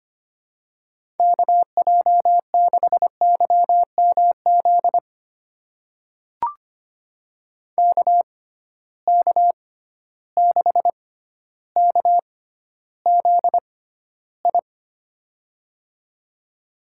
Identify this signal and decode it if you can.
Morse code, heard as a keyed tone, about 25 wpm, spaced as for 13 wpm, 700 Hz. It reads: KJ6YMZ KK6KZI
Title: USA Call Signs - 6 Characters - 6x Character Spacing - 25wpm